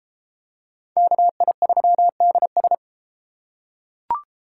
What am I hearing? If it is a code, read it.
Morse code, heard as a keyed tone, about 32 wpm, 700 Hz. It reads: KI3DS